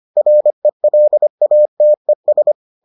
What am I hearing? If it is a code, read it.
Morse code, heard as a keyed tone, about 25 wpm, 600 Hz. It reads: RELATES